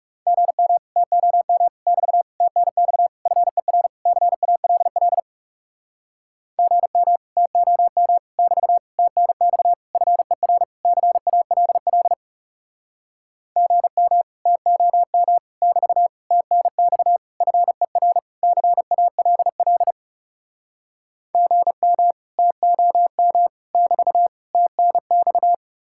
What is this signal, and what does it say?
Morse code, heard as a keyed tone, about 40 wpm, 700 Hz. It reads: GM TOM <BT> TNX FER CALL GM TOM <BT> TNX FER CALL GM TOM <BT> TNX FER CALL GM TOM <BT> TNX